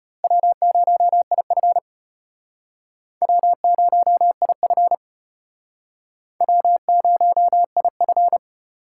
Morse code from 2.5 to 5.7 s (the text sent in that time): W0IF